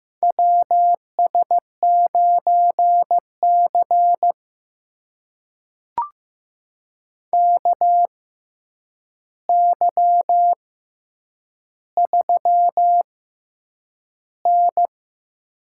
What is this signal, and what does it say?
Morse code, heard as a keyed tone, about 15 wpm, spaced as for 8 wpm, 700 Hz. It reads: WS9C KY3N
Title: USA Call Signs - 4 Characters - 6x Character Spacing - 15wpm